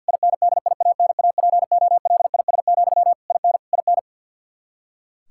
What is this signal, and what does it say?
Morse code, heard as a keyed tone, about 52 wpm, 700 Hz. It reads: INDIANAPOLIS, IN IN
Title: Practice QSO - Exchange 1 - Beginning - 50wpm